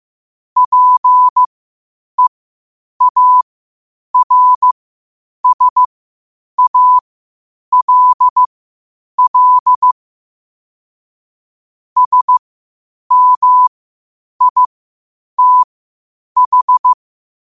Morse code, heard as a keyed tone, about 15 wpm, spaced as for 8 wpm, 1000 Hz. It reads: PEARSALL SMITH
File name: qotdfiles / 8wpm